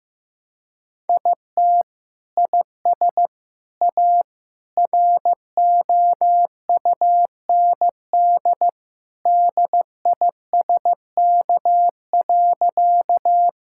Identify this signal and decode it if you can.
Morse code, heard as a keyed tone, about 15 wpm, 700 Hz. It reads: IT IS A ROUND DISK.
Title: Sentences from Top 1000 Words - Review Part 6 - 15wpm